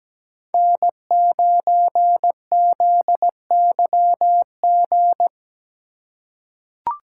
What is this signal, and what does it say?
Morse code, heard as a keyed tone, about 17 wpm, 700 Hz. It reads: N9ZYG
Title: USA Call Signs - 5 Characters - 4x Character Spacing - 17wpm